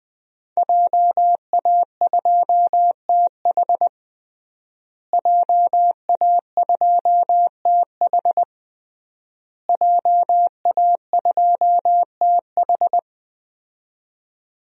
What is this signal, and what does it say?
Morse code, heard as a keyed tone, about 20 wpm, 700 Hz. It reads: JA2TH JA2TH JA2TH
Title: DXCC-100 Call Signs - 5 Characters - Repeated 3x - 20wpm